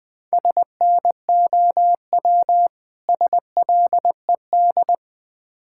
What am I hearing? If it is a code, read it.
Morse code, heard as a keyed tone, about 21 wpm, 700 Hz. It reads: SNOW SLED
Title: Difficult Words - Speed-Racing with Hints - 20wpm